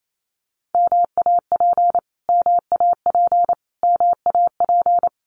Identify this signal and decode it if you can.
Morse code, heard as a keyed tone, about 28 wpm, 700 Hz. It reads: MAP MAP MAP
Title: Top 500 Words Repeated 28wpm